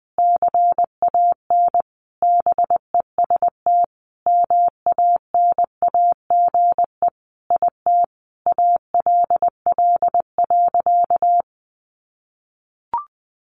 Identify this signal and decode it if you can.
Morse code, heard as a keyed tone, about 20 wpm, 700 Hz. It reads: CAN BEST MANAGE IT ALL.
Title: Sentences from Top 1400 Words 20wpm